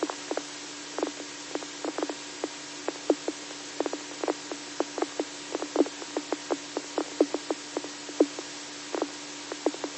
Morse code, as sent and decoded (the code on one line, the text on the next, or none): none